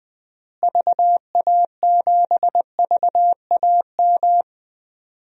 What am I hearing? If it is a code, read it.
Morse code, heard as a keyed tone, about 20 wpm, 700 Hz. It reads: VA7VAM
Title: DXCC-100 Call Signs - 6 Characters - Repeated 2x - 20wpm